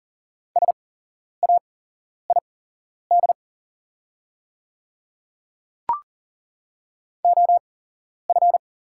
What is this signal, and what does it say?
Morse code, heard as a keyed tone, about 40 wpm, spaced as for 11 wpm, 700 Hz. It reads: SAID OF
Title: 8x Character Spacing - Top 100 Words - 40wpm